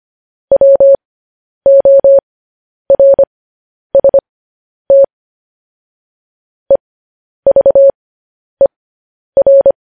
Morse code, heard as a keyed tone, about 25 wpm, spaced as for 10 wpm, 550 Hz. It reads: WORST EVER
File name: podcast / News headlines-1x-25x10